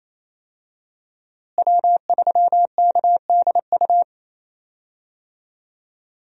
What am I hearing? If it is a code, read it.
Morse code, heard as a keyed tone, about 28 wpm, 700 Hz. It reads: W3KDU
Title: USA Call Signs - 5 Characters - Repeated 3x - 28wpm